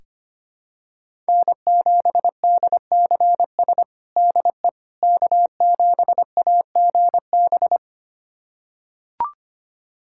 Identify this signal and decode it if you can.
Morse code, heard as a keyed tone, about 25 wpm, 700 Hz. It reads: N7DCS DE K7AGB
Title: USA Callsign DE Callsign 25wpm